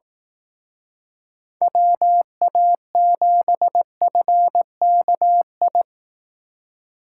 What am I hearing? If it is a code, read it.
Morse code, heard as a keyed tone, about 19 wpm, 700 Hz. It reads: WA7FKI
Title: Speed Racing - USA 2x3 Call Signs 15wpm